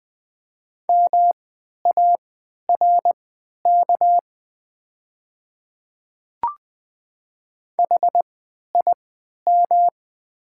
Morse code, heard as a keyed tone, about 20 wpm, spaced as for 11 wpm, 700 Hz. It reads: MARK HIM
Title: 3x Character Spacing - Top 300 Words - 20wpm